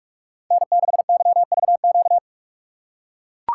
Morse code, heard as a keyed tone, about 42 wpm, 700 Hz. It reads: N6YVQ
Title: USA Call Signs - 5 Characters - 2x Character Spacing - 45wpm